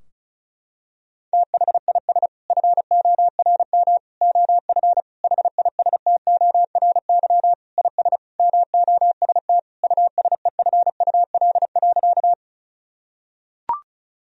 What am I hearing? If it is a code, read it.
Morse code, heard as a keyed tone, about 34 wpm, 700 Hz. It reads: THIS FORM OF HISTORY IS MOST USEFUL.
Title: Sentences from Top 1000 Words - Review Part 4 - 35wpm